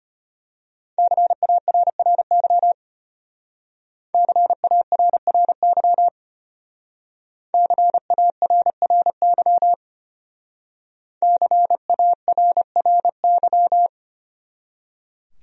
Morse code, heard as a keyed tone, about 29 wpm, 700 Hz. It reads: CARRY CARRY CARRY CARRY
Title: Speed-Racing Top 500 Words 25wpm